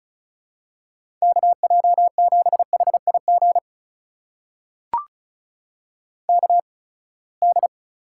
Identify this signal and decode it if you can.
Morse code, heard as a keyed tone, about 34 wpm, spaced as for 24 wpm, 700 Hz. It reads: KJ7HIG KD
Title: USA Call Signs - 6 Characters - 8x Character Spacing - 35wpm